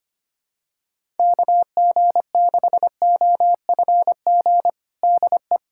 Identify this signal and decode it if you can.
Morse code, heard as a keyed tone, about 25 wpm, 700 Hz. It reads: KG6OFG DE
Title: USA Callsign DE Callsign 25wpm